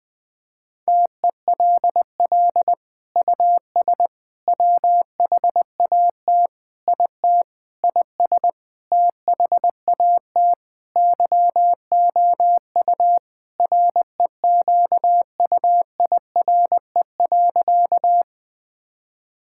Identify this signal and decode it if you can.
Morse code, heard as a keyed tone, about 20 wpm, 700 Hz. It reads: TELL US WHAT IT IS THAT YOU REQUIRE.
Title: Sentences from Top 1000 Words - Review Part 6 - 20wpm